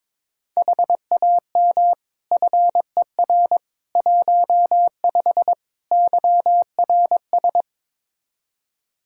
Morse code, heard as a keyed tone, about 22 wpm, 700 Hz. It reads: HAM FER 15 YRS
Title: Practice QSO - Exchange 3 - More Info - 22wpm